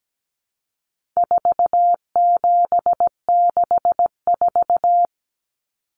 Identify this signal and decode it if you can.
Morse code, heard as a keyed tone, about 17 wpm, 700 Hz. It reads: 4764
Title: Numbers - 4-Digit 17wpm